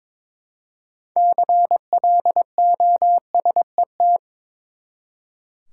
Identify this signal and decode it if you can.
Morse code, heard as a keyed tone, about 22 wpm, 700 Hz. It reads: CLOSET
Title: Six Letter Words - 22wpm